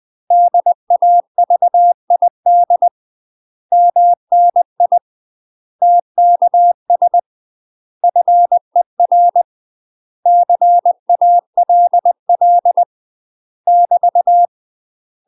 Morse code, heard as a keyed tone, about 20 wpm, 700 Hz. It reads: DAVID MNI TKS FER CALL <BT>
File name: UK MASTER QSOs TRAINING / 20WPM / UK MASTER QSOs 0065